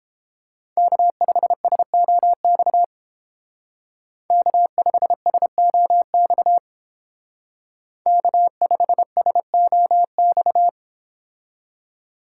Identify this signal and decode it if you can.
Morse code, heard as a keyed tone, about 31 wpm, 700 Hz. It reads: K5SOX K5SOX K5SOX